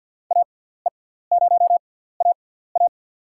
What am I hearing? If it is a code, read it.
Morse code, heard as a keyed tone, about 47 wpm, spaced as for 17 wpm, 700 Hz. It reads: AE0AA
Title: USA Call Signs - 5 Characters - 6x Character Spacing - 50wpm